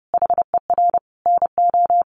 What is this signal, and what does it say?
Morse code, heard as a keyed tone, about 30 wpm, 700 Hz. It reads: HER NO
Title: Sentences from Top 600 Words 30wpm